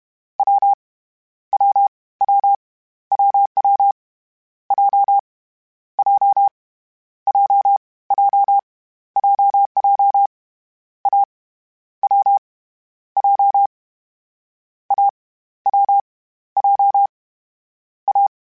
Morse code, heard as a keyed tone, about 31 wpm, 800 Hz. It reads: W W W WW J J J J JJ A W J A W J A